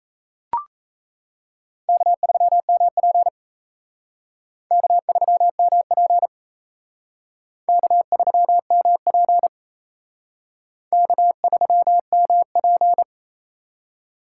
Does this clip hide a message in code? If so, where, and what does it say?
Yes: Morse code from 1.9 to 13.0 s, heard as a keyed tone, about 36 wpm, 700 Hz. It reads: K3MP K3MP K3MP K3MP